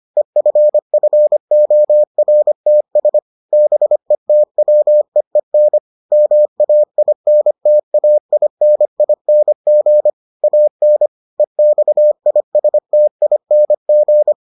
Morse code, heard as a keyed tone, about 25 wpm, 600 Hz. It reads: EFFORTS BETWEEN MAINTAINING AN EXISTING